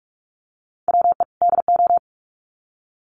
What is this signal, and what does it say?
Morse code, heard as a keyed tone, about 44 wpm, 700 Hz. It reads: WE DO
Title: Sets of 2 Words 45wpm